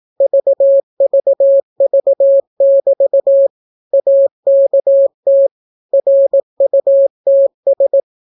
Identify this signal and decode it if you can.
Morse code, heard as a keyed tone, about 18 wpm, 550 Hz. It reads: VVV<BT> AKT RUTS